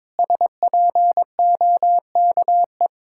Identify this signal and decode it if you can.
Morse code, heard as a keyed tone, about 22 wpm, 700 Hz. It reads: SPOKE
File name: Sentences with 7-Word Phrases - 22wpm